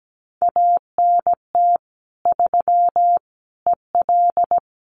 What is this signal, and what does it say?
Morse code, heard as a keyed tone, about 17 wpm, 700 Hz. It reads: ANT 3 EL
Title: Common QSO Phrases Encore 17wpm